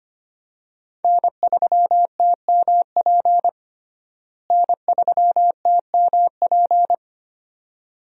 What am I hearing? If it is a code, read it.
Morse code, heard as a keyed tone, about 25 wpm, 700 Hz. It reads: N3TMP N3TMP
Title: USA Call Signs - 5 Characters - Repeated 2x - 25wpm